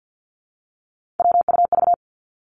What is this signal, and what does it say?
Morse code, heard as a keyed tone, about 50 wpm, 700 Hz. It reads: WUV
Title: Three Challenging Letter-Number Combinations 50wpm